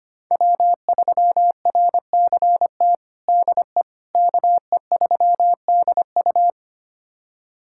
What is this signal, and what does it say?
Morse code, heard as a keyed tone, about 25 wpm, 700 Hz. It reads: W3RCT DE KE3DU
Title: USA Callsign DE Callsign - 5 Characters - 25wpm